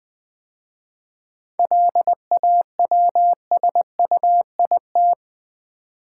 Morse code, heard as a keyed tone, about 20 wpm, 690 Hz. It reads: LAWSUIT
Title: 7 Letter Words - First 3 Letters Given - 20wpm